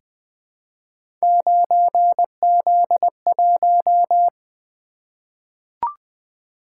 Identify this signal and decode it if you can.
Morse code, heard as a keyed tone, about 20 wpm, 700 Hz. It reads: 9Z1
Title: Three Letter-Number Combinations - 20wpm